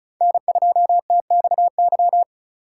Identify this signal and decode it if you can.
Morse code, heard as a keyed tone, about 45 wpm, 700 Hz. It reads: N2TXY